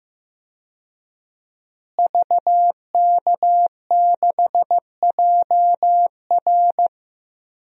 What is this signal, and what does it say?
Morse code, heard as a keyed tone, about 15 wpm, 700 Hz. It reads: VK6JR
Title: DXCC-100 Call Signs - 5 Characters - Repeated 2x - 15wpm